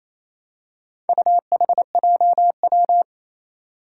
Morse code, heard as a keyed tone, about 27 wpm, 700 Hz. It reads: UHJW
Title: Four Challenging Letter-Number Combinations 28wpm